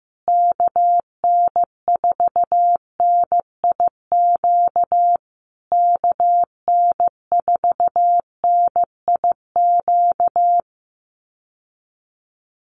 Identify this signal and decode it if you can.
Morse code, heard as a keyed tone, about 15 wpm, 700 Hz. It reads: KN4NIQ KN4NIQ
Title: USA Call Signs Repeated 3x 15wpm